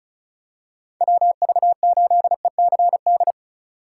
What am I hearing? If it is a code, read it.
Morse code, heard as a keyed tone, about 36 wpm, 700 Hz. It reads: WV8ECD
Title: USA Call Signs - 2x Character Spacing - 35wpm